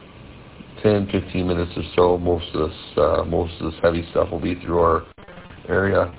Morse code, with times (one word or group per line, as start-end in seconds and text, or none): none